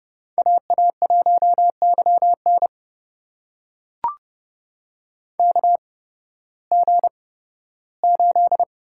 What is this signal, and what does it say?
Morse code, heard as a keyed tone, about 30 wpm, spaced as for 16 wpm, 700 Hz. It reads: AA1YN KG8
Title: USA Call Signs - 5 Characters - 8x Character Spacing - 30wpm